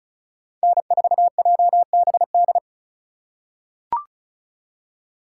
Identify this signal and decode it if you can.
Morse code, heard as a keyed tone, about 36 wpm, 700 Hz. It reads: N4JBD
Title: USA Call Signs - 5 Characters - 2x Character Spacing - 35wpm